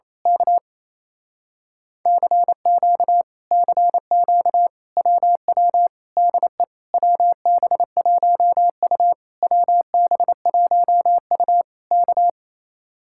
Morse code, heard as a keyed tone, about 27 wpm, 700 Hz. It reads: K CQ CQ WW DE WB1U WB1U K
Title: Speed-Racing Common QSO Phrases Encore - 28wpm